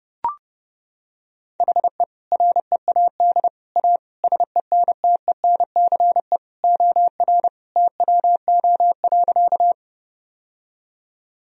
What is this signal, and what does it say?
Morse code, heard as a keyed tone, about 30 wpm, 700 Hz. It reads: HE READ A SENTENCE OR TWO.
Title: Sentences from Top 1000 Words - Review Part 1 - 30wpm